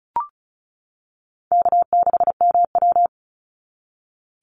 Morse code, heard as a keyed tone, about 34 wpm, 700 Hz. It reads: K6MW
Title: NA SOTA Chaser Callsigns 35wpm